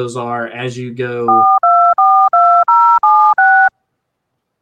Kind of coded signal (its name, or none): DTMF